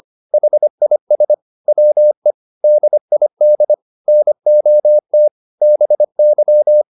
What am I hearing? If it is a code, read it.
Morse code, heard as a keyed tone, about 25 wpm, 600 Hz. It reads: HIS WE DID NOT BY
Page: advanced-practice-files